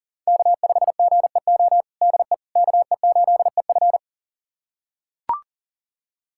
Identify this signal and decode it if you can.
Morse code, heard as a keyed tone, about 40 wpm, 700 Hz. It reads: K5GEO DE KE8EF
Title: USA Callsign DE Callsign - 5 Characters - 40wpm